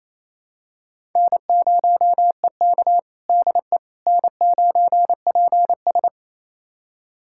Morse code, heard as a keyed tone, about 28 wpm, 700 Hz. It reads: N0EK DE N9PS
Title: USA Callsign DE Callsign - 4 Characters - 28wpm